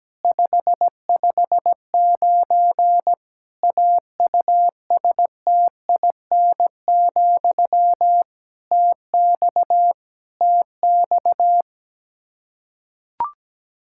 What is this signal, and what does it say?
Morse code, heard as a keyed tone, about 17 wpm, 700 Hz. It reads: 559 AUSTIN, TX TX
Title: K1USN SST QSO Practice 17wpm